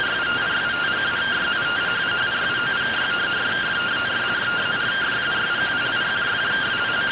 Two-tone teleprinter signal, about 170 Hz shift.